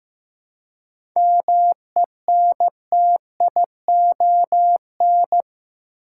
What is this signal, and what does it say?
Morse code, heard as a keyed tone, about 16 wpm, 700 Hz. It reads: MENTION
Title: Seven Letter Words - 15wpm